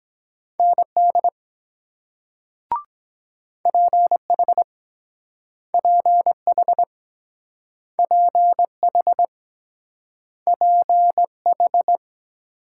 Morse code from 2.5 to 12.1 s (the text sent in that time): PH PH PH PH